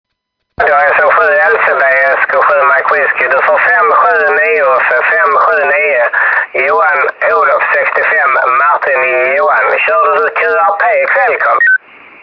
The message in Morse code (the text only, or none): none